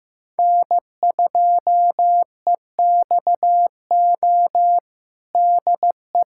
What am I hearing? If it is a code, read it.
Morse code, heard as a keyed tone, about 15 wpm, 700 Hz. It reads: N2EXO DE
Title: USA Callsign DE Callsign - 5 Characters - 15wpm